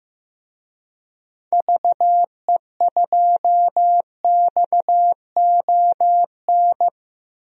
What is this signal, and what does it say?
Morse code, heard as a keyed tone, about 15 wpm, 700 Hz. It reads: VE2XON
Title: DXCC-100 Call Signs - 6 Characters - Repeated 2x - 15wpm